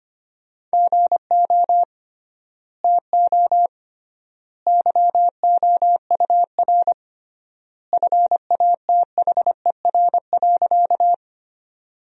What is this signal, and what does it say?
Morse code, heard as a keyed tone, about 25 wpm, 700 Hz. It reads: GO TO YOUR FATHER.
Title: 3x Word-Spacing - Sentences from Top 300 Words - 25wpm